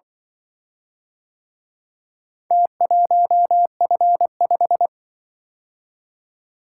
Morse code, heard as a keyed tone, about 24 wpm, 700 Hz. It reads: T1F5
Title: Speed-Racing - Four Letter-Number Combinations - 20wpm